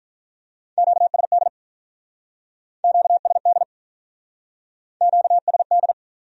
Morse code, heard as a keyed tone, about 45 wpm, 700 Hz. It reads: QSD QSD QSD